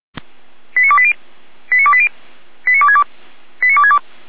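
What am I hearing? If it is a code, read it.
Five-tone selective call (ZVEI), heard as quick runs of five tones, tones of about 70 ms. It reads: 89280 89280 89262 89262
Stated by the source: ZVEI five-tone selective call 89304, received by SDR, from the server of feuerwehr-orlamuende (fire brigade Orlamünde).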